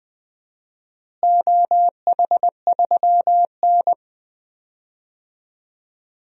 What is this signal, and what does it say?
Morse code, heard as a keyed tone, about 21 wpm, 700 Hz. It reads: OH3N
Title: DXCC-100 Call Signs - 4 Characters - Repeated 3x - 20wpm